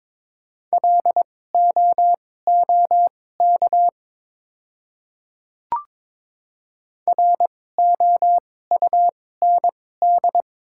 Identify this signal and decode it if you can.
Morse code, heard as a keyed tone, about 22 wpm, spaced as for 16 wpm, 690 Hz. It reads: LOOK ROUND
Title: 2x Character Spacing - Top 300 Words - 22wpm